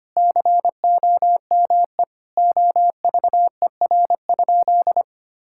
Morse code, heard as a keyed tone, about 25 wpm, 700 Hz. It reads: COME OVER?
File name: practice / 25wpm / Sentences with 4-Word Phrases - 25wpm